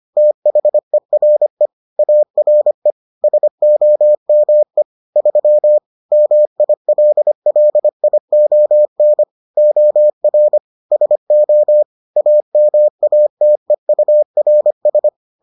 Morse code, heard as a keyed tone, about 25 wpm, 600 Hz. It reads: THERE ARE SOME 3 MILLION OR SO AMATEURS